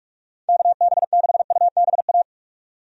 Morse code, heard as a keyed tone, about 46 wpm, 700 Hz. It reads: KB6UBA
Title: USA Call Signs - 6 Characters - 45wpm